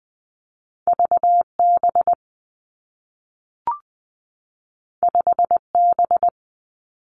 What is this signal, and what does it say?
Morse code, heard as a keyed tone, about 20 wpm, 700 Hz. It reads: V B 5 B
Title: Two Challenging Letter-Number Combinations 20wpm